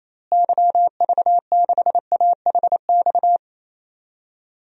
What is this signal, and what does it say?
Morse code, heard as a keyed tone, about 28 wpm, 700 Hz. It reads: YV6AHX
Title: DXCC-100 Call Signs - 6 Characters - Repeated 3x - 28wpm